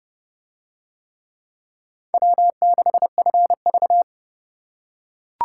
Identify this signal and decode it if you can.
Morse code, heard as a keyed tone, about 29 wpm, 700 Hz. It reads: W6FV